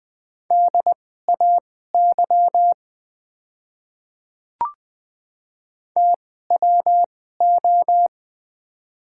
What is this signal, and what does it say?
Morse code, heard as a keyed tone, about 20 wpm, spaced as for 14 wpm, 690 Hz. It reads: DAY TWO
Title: 2x Character Spacing - Top 100 Words - 20wpm